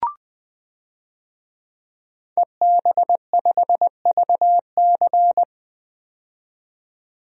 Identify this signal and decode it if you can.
Morse code, heard as a keyed tone, about 20 wpm, 700 Hz. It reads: EB5VC
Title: DXCC-100 Call Signs - 5 Characters - 20wpm